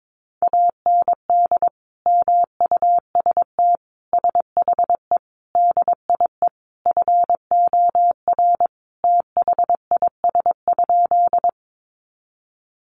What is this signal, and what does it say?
Morse code, heard as a keyed tone, about 22 wpm, 700 Hz. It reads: AND MUST SHE DIE FOR THIS?
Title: Sentences from Top 1300 Words 22wpm